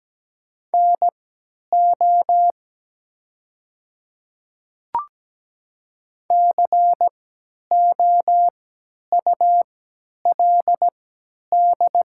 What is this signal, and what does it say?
Morse code, heard as a keyed tone, about 17 wpm, spaced as for 10 wpm, 700 Hz. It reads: NO COULD
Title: 3x Character Spacing - Top 100 Words - 17wpm